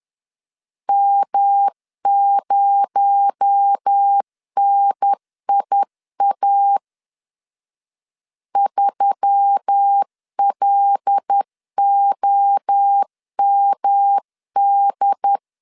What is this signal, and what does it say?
Morse code, heard as a keyed tone, about 10 wpm, 800 Hz. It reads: M0NIA 3LOMD
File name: LEZIONI / 7 GF9Z / 7 G 10wpm 2501